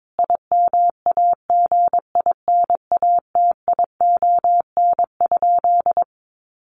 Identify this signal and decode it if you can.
Morse code, heard as a keyed tone, about 22 wpm, 700 Hz. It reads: IMAGINATION?
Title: Sentences from Top 1300 Words 22wpm